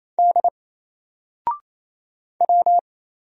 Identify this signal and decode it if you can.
Morse code, heard as a keyed tone, about 27 wpm, 700 Hz. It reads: D W
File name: Single-Letter-Number-28wpm